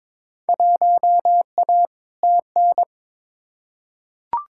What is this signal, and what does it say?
Morse code, heard as a keyed tone, about 22 wpm, 700 Hz. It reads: 1A TN
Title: Field Day Exchanges - 22wpm